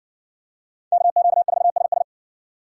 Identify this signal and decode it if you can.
Morse code, heard as a keyed tone, about 60 wpm, 700 Hz. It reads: KQ4SS